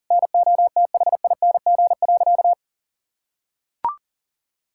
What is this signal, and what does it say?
Morse code, heard as a keyed tone, about 40 wpm, 700 Hz. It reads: NOTHING.
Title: Sentences from Top 1000 Words - Review Part 6 - 40wpm